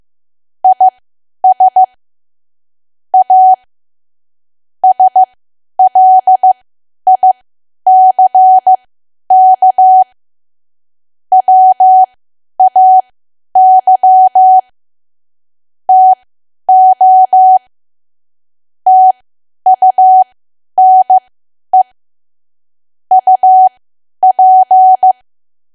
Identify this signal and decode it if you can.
Morse code, heard as a keyed tone, about 15 wpm, spaced as for 10 wpm, 750 Hz. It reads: IS A SLICK WAY TO TUNE UP
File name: cwa / W1AW-Level2 / 10WPM